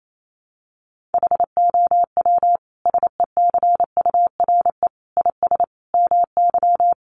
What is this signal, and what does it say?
Morse code, heard as a keyed tone, about 28 wpm, 700 Hz. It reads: HOW SECURE IS MY
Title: Sentences from Top 1200 Words 28wpm